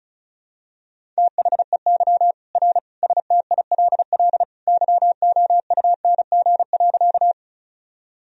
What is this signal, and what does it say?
Morse code, heard as a keyed tone, about 36 wpm, 700 Hz. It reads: THEY R STILL YOUNG.
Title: Sentences from Top 400 Words and Abbreviations 35wpm